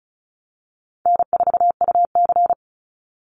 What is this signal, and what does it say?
Morse code, heard as a keyed tone, about 34 wpm, 700 Hz. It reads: N4UC
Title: Call Signs - 35wpm